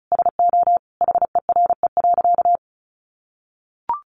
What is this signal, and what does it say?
Morse code, heard as a keyed tone, about 34 wpm, 700 Hz. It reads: SO HERE.
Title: Sentences from Top 600 Words 35wpm